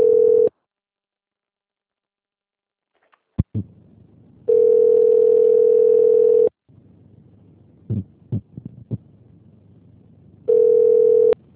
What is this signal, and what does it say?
Ringback tone (North America), 440 + 480 Hz.
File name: out- - - - - .2171